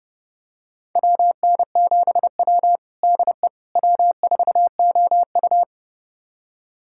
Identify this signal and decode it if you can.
Morse code, heard as a keyed tone, about 30 wpm, 700 Hz. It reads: WN7W DE W4OU
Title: USA Callsign DE Callsign - 4 Characters - 30wpm